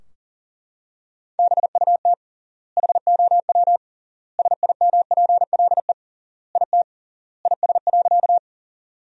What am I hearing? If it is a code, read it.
Morse code, heard as a keyed tone, about 40 wpm, 700 Hz. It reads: BUT HOW SIMPLE IT IS.
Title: 3x Word-Spacing - Sentences from Top 400 Words - 40wpm